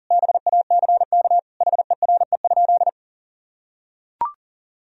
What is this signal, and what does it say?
Morse code, heard as a keyed tone, about 40 wpm, 700 Hz. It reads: BACK HERE?
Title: Sentences from Top 1000 Words - Review Part 4 - 40wpm